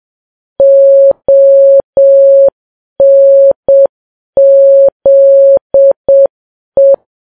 Morse code, heard as a keyed tone, about 7 wpm, 550 Hz. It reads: ONZE
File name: Files Web tls / Morse on9fv / morse7kps